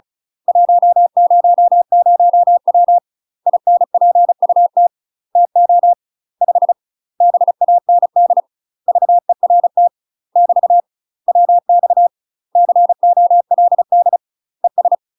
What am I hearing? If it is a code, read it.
Morse code, heard as a keyed tone, about 34 wpm, 700 Hz. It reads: 100W INPUT TO 5 BAND VERT <BT> WX COLD ES
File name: UK MASTER QSOs TRAINING / 35WPM / UK MASTER QSOs 0003